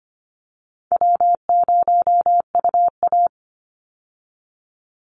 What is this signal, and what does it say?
Morse code, heard as a keyed tone, about 25 wpm, 700 Hz. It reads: W0UA
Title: Easy Call Signs Repeated 2x - 25wpm